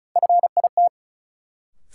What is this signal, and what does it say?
Morse code, heard as a keyed tone, about 34 wpm, 700 Hz. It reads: FIT
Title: Sentences from Top 800 Words - New Word Drill - 35wpm